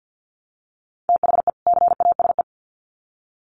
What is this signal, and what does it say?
Morse code, heard as a keyed tone, about 50 wpm, 700 Hz. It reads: THE CASE
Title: Sets of 2 Words 50wpm